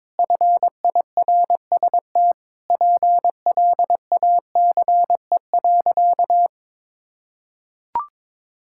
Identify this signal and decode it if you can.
Morse code, heard as a keyed tone, about 22 wpm, 700 Hz. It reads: FIRST PLACE.
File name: Sentences with 7-Word Phrases - 22wpm